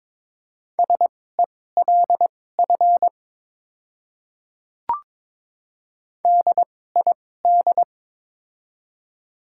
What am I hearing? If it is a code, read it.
Morse code, heard as a keyed tone, about 22 wpm, spaced as for 16 wpm, 700 Hz. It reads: SELF DID
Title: 2x Character Spacing - Top 200 Words - 22wpm